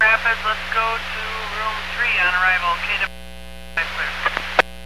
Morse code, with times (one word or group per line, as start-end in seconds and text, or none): none